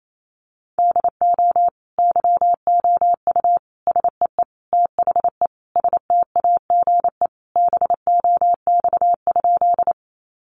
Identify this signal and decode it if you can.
Morse code, heard as a keyed tone, about 28 wpm, 700 Hz. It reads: DO YOU SEE THE STAGE BOX?
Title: Sentences from Top 1200 Words 28wpm